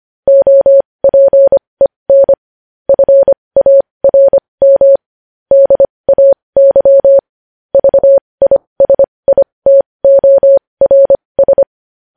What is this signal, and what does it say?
Morse code, heard as a keyed tone, about 25 wpm, spaced as for 20 wpm, 550 Hz. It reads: OPEN FARM DAY VISITORS
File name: podcast / News headlines-1x-25x20